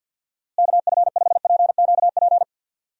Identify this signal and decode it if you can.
Morse code, heard as a keyed tone, about 51 wpm, 700 Hz. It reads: KF5PQP
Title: USA Call Signs - 6 Characters - Repeated 2x - 50wpm